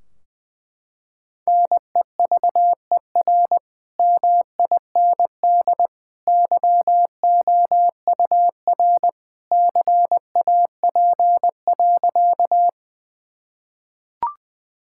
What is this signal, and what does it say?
Morse code, heard as a keyed tone, about 20 wpm, 700 Hz. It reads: NEVER MIND YOUR CAP.
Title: Sentences from Top 1600 Words 20wpm